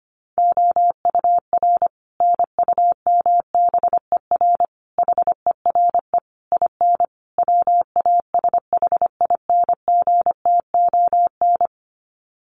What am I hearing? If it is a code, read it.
Morse code, heard as a keyed tone, about 25 wpm, 700 Hz. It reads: OUR NUMBER HERE IN WASHINGTON
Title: Sets of 5 Words Encore2 25wpm